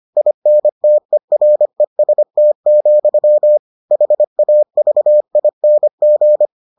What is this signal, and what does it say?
Morse code, heard as a keyed tone, about 25 wpm, 600 Hz. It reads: INTEREST, HAVING